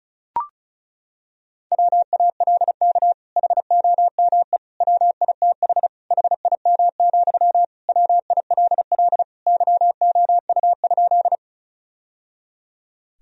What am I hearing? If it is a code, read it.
Morse code, heard as a keyed tone, about 36 wpm, 690 Hz. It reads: WALK HOME WITH HIM, WILL YOU?
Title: Sentences from Top 1000 Words - Review Part 1 - 35wpm